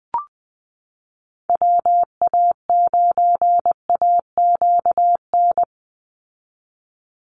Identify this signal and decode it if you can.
Morse code, heard as a keyed tone, about 20 wpm, 700 Hz. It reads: WA9AQN
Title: Call Signs Repeated 2x - 20wpm